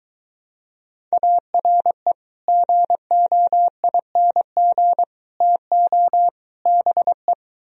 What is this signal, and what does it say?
Morse code, heard as a keyed tone, about 23 wpm, 700 Hz. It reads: ARE GOING TO BE